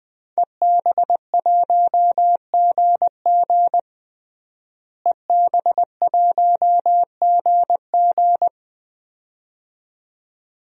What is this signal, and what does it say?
Morse code, heard as a keyed tone, about 20 wpm, 700 Hz. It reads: EB1GG EB1GG
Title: DXCC-100 Call Signs - 5 Characters - Repeated 2x - 20wpm